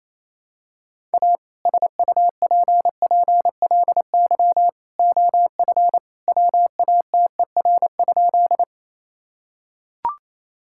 Morse code, heard as a keyed tone, about 28 wpm, 700 Hz. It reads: A SUPPLY OF WATER?
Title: Sentences from Top 1000 Words - Review Part 5 - 28wpm